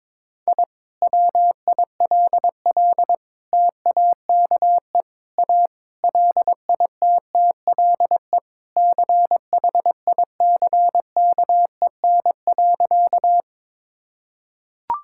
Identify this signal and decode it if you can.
Morse code, heard as a keyed tone, about 22 wpm, 700 Hz. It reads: I WILL TAKE A LITTLE CHICKEN.
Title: Sentences from Top 1000 Words - Review Part 5 - 22wpm